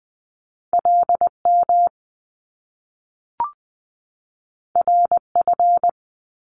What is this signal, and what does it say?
Morse code, heard as a keyed tone, about 20 wpm, 700 Hz. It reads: LM RF
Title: Top Two Letter Combinations in Words 20wpm